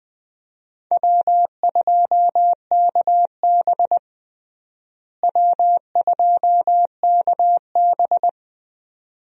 Morse code, heard as a keyed tone, about 21 wpm, 700 Hz. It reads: W2KB W2KB
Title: USA Call Signs - 4 Characters - Repeated 2x - 20wpm